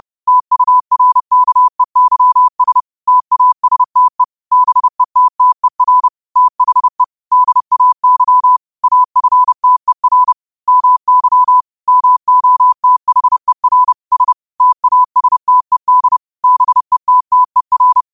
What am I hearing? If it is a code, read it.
Morse code, heard as a keyed tone, about 30 wpm, 1000 Hz. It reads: TURKEYS TASTE BETTER THE DAY AFTER MY MOTHER S TASTED BETTER